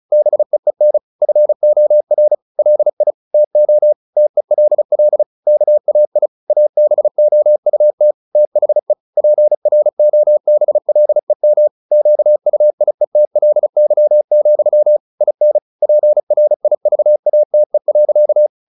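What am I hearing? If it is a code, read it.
Morse code, heard as a keyed tone, about 34 wpm, 600 Hz. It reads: BEEN FOR LI TO TELL KAI ABOUT THE PROBLEM QUIETLY, IN PRIVATE.
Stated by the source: Morse practice at 35 wpm from CW med Gnister, sending Ken Follett's novel Never in English.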